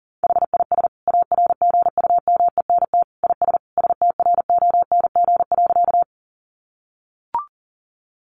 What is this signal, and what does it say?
Morse code, heard as a keyed tone, about 40 wpm, 700 Hz. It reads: HIS ARGUMENT IS STRONG.
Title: Sentences from Top 1400 Words 40wpm